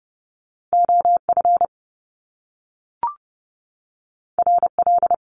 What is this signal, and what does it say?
Morse code, heard as a keyed tone, about 30 wpm, 700 Hz. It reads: OF RL
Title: Top Two Letter Combinations in Words 30wpm